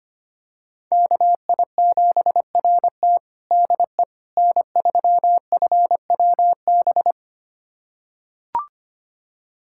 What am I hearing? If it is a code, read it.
Morse code, heard as a keyed tone, about 25 wpm, 700 Hz. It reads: KI7RT DE N3FWB
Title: USA Callsign DE Callsign - 5 Characters - 25wpm